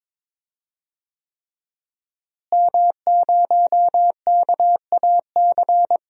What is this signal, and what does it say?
Morse code, heard as a keyed tone, about 22 wpm, 700 Hz. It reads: M0KAC
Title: DXCC-100 Call Signs - 5 Characters - Repeated 3x - 22wpm